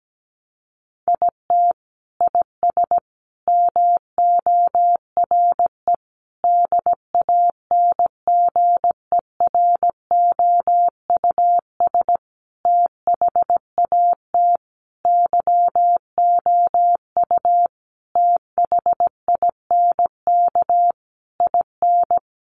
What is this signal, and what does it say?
Morse code, heard as a keyed tone, about 17 wpm, 700 Hz. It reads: IT IS MORE DANGEROUS THAT YOU THINK IN
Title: Sentences from Top 1200 Words 17wpm